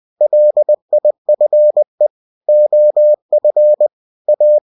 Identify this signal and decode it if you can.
Morse code, heard as a keyed tone, about 20 wpm, 600 Hz. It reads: LIFE OF A